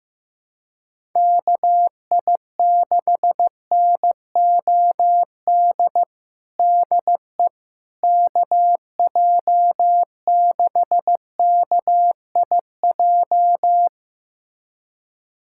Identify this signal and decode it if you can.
Morse code, heard as a keyed tone, about 15 wpm, 700 Hz. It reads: KI6NOD DE KJ6KIJ
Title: USA Callsign DE Callsign - 6 Characters - 15wpm